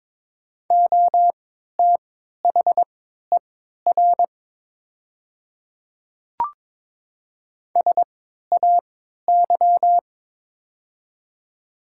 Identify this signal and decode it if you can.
Morse code, heard as a keyed tone, about 22 wpm, spaced as for 13 wpm, 700 Hz. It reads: OTHER SAY
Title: 3x Character Spacing - Top 200 Words - 22wpm